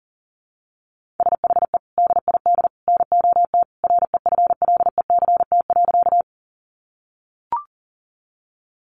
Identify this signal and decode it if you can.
Morse code, heard as a keyed tone, about 40 wpm, 700 Hz. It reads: SHE DID NOT REFLECT.
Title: Sentences from Top 1500 Words 40wpm